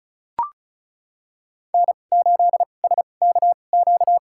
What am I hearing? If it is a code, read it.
Morse code, heard as a keyed tone, about 36 wpm, spaced as for 25 wpm, 700 Hz. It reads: N8SKQ
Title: USA Call Signs - 5 Characters - 2x Character Spacing - 35wpm